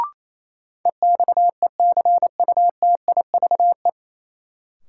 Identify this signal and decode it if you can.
Morse code, heard as a keyed tone, about 28 wpm, 700 Hz. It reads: EXECUTIVE
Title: Sets of 5 Words Encore - New Word Drill - 28wpm